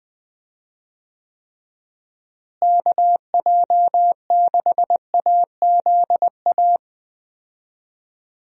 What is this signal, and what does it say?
Morse code, heard as a keyed tone, about 20 wpm, 700 Hz. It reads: KJ6AZA
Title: USA Call Signs - 6 Characters - Repeated 3x - 20wpm